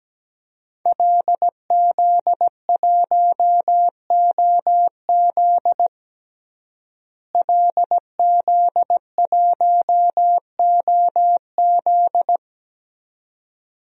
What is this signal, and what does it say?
Morse code, heard as a keyed tone, about 17 wpm, 700 Hz. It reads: LZ1OZ LZ1OZ
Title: DXCC-100 Call Signs - 5 Characters - Repeated 2x - 17wpm